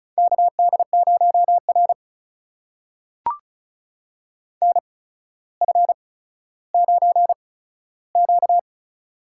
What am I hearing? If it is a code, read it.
Morse code, heard as a keyed tone, about 36 wpm, spaced as for 15 wpm, 700 Hz. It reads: KD0R NF9Q
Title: USA Call Signs - 4 Characters - 8x Character Spacing - 35wpm